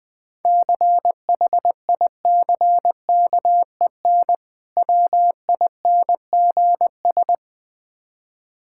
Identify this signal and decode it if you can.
Morse code, heard as a keyed tone, about 20 wpm, 700 Hz. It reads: CHICKEN WINGS
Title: Difficult Words with Hints - 20wpm